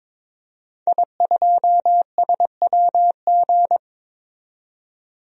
Morse code, heard as a keyed tone, about 22 wpm, 700 Hz. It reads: I2SWG
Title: DXCC-100 Call Signs - 5 Characters - 22wpm